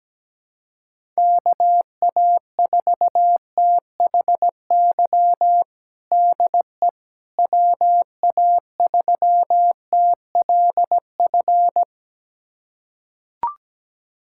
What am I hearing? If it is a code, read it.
Morse code, heard as a keyed tone, about 17 wpm, 700 Hz. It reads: KA4THY DE WA3TLF
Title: USA Callsign DE Callsign 17wpm